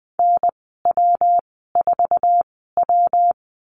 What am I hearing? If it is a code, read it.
Morse code, heard as a keyed tone, about 20 wpm, spaced as for 14 wpm, 700 Hz. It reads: NW4W
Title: 147 Lesson 24g - Letter y - Call Signs 20wpm